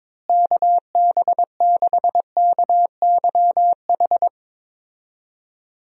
Morse code, heard as a keyed tone, about 22 wpm, 700 Hz. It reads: KB6KYH
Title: USA Call Signs - 6 Characters - Repeated 2x - 22wpm